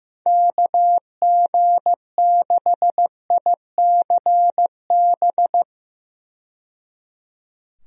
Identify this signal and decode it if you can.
Morse code, heard as a keyed tone, about 15 wpm, 700 Hz. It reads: KG6ICB
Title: USA Call Signs - 6 Characters - 15wpm